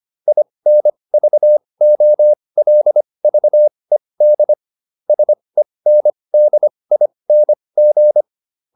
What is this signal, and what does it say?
Morse code, heard as a keyed tone, about 25 wpm, spaced as for 20 wpm, 600 Hz. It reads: INVOLVED SENDING
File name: SS215 20